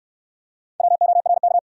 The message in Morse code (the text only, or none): WORK